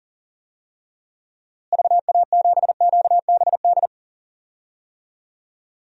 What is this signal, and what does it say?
Morse code, heard as a keyed tone, about 38 wpm, 700 Hz. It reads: VA7QBD
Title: DXCC-100 Call Signs - 6 Characters - Repeated 3x - 40wpm